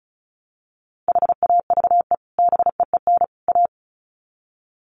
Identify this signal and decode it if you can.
Morse code, heard as a keyed tone, about 34 wpm, 700 Hz. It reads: HAVE BEEN A
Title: Sets of 3 Words 35wpm